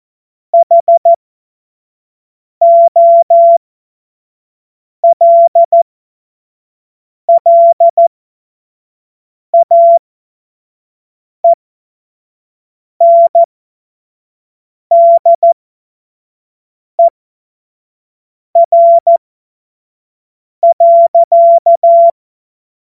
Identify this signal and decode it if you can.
Morse code, heard as a keyed tone, about 14 wpm, spaced as for 5 wpm, 680 Hz. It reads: HOLLAENDER.
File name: Morsekurs / Witze / Witz 20. Januar 2026 25 F70